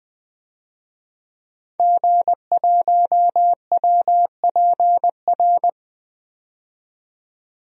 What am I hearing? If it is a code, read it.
Morse code, heard as a keyed tone, about 20 wpm, 700 Hz. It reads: G1WPR
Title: DXCC-100 Call Signs - 5 Characters - 20wpm